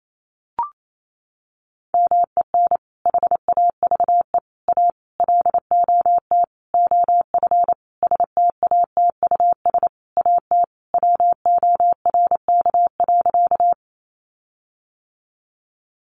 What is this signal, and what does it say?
Morse code, heard as a keyed tone, about 28 wpm, 700 Hz. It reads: MEN HAVE A LOT OF STATUS AT WORK.
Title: Sentences from Top 1400 Words 28wpm